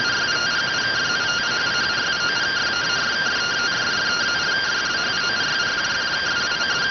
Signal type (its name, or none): two-tone teleprinter signal